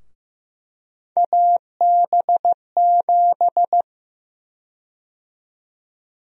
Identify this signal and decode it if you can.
Morse code, heard as a keyed tone, about 15 wpm, 700 Hz. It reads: AB7
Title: USA Call Sign Prefixes - Repeated 2x - 15wpm